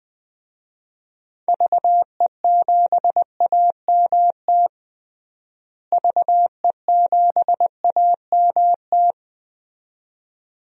Morse code, heard as a keyed tone, about 20 wpm, 700 Hz. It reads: VE7AMT VE7AMT
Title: DXCC-100 Call Signs - 6 Characters - Repeated 2x - 20wpm